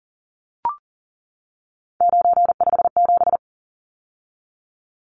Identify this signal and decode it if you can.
Morse code, heard as a keyed tone, about 40 wpm, 700 Hz. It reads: 957
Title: Numbers - 3-Digit 40wpm